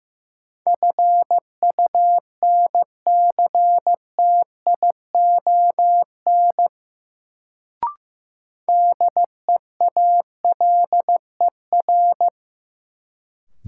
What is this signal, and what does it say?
Morse code, heard as a keyed tone, about 15 wpm, 700 Hz. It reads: FUNCTION DEALER
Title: Sentences from Top 1100 Words - New Word Drill - 15wpm